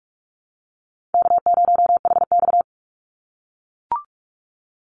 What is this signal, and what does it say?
Morse code, heard as a keyed tone, about 44 wpm, 700 Hz. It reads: K0HX
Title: Easy Call Signs - 45wpm